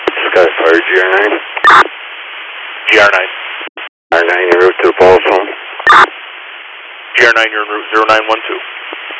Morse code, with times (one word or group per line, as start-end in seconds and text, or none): none